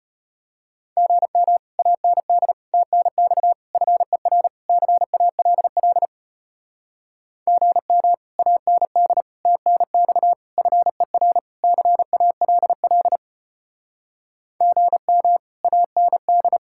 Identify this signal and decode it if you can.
Morse code, heard as a keyed tone, about 33 wpm, 700 Hz. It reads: GM AND TNX FER CALL GM AND TNX FER CALL GM AND